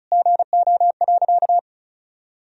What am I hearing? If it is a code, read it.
Morse code, heard as a keyed tone, about 34 wpm, 700 Hz. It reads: GO.
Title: Sentences from Top 1000 Words - Review Part 2 - 35wpm